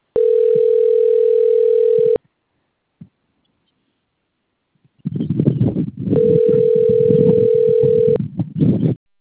Ringback tone (North America), 440 + 480 Hz.